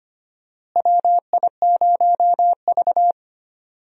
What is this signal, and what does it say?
Morse code, heard as a keyed tone, about 25 wpm, 700 Hz. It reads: WI0V
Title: USA Call Signs - 4 Characters - Repeated 2x - 25wpm